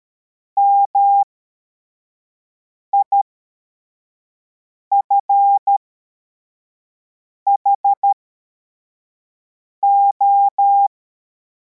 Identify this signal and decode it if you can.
Morse code, heard as a keyed tone, about 13 wpm, spaced as for 4 wpm, 800 Hz. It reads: MIFHO
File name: formation / cw / Text74